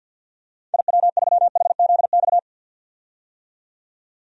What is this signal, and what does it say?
Morse code, heard as a keyed tone, about 52 wpm, 700 Hz. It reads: IW3HZX
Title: DXCC-100 Call Signs - 6 Characters - 50wpm